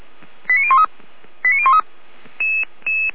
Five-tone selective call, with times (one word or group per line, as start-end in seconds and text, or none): 0.5-0.8 89013
1.4-1.8 89013